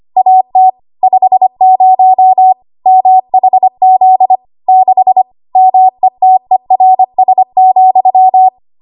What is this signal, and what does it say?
Morse code, heard as a keyed tone, about 25 wpm, 750 Hz. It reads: AT 50 MHZ 6 METERS,
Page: W1AW Web Code Practice Files